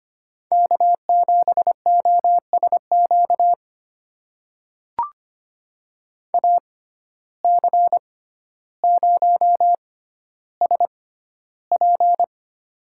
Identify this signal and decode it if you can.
Morse code, heard as a keyed tone, about 25 wpm, spaced as for 13 wpm, 700 Hz. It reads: K7OSQ AC0SP
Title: USA Call Signs - 5 Characters - 6x Character Spacing - 25wpm